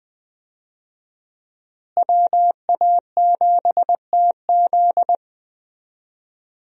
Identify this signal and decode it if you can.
Morse code, heard as a keyed tone, about 20 wpm, 700 Hz. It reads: WA7TZ
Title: USA Call Signs - 5 Characters - Repeated 2x - 20wpm